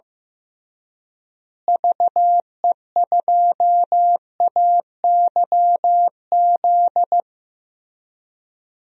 Morse code, heard as a keyed tone, about 15 wpm, 700 Hz. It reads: VE2AYZ
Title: DXCC-100 Call Signs - 6 Characters - Repeated 3x - 15wpm